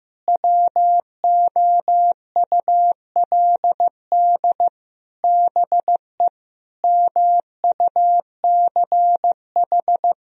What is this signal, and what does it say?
Morse code, heard as a keyed tone, about 15 wpm, 700 Hz. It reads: WOULD BE MUCH
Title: Sentences from Top 1000 Words - Review Part 3 - 15wpm